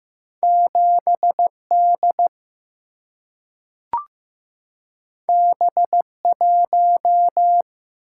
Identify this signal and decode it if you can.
Morse code, heard as a keyed tone, about 15 wpm, 700 Hz. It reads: 7 D B 1
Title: Single-Letter Single-Number 15wpm